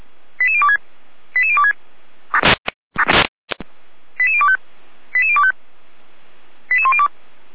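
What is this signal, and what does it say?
Five-tone selective call (ZVEI), heard as quick runs of five tones, tones of about 70 ms. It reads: 88026 88026 88025 88025 88102
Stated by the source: ZVEI five-tone call received by SDR.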